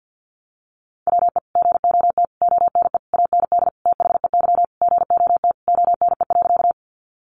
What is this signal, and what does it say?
Morse code, heard as a keyed tone, about 50 wpm, 700 Hz. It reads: WE GOT ONE AND THEY GOT ONE.
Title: Sentences from Top 300 Words 50wpm